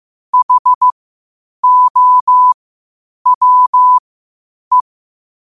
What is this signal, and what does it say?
Morse code, heard as a keyed tone, about 15 wpm, spaced as for 8 wpm, 1000 Hz. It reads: HOWE
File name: qotdfiles / 8wpm